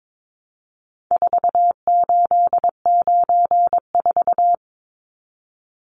Morse code, heard as a keyed tone, about 22 wpm, 700 Hz. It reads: 4894
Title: Numbers - 4-Digit 22wpm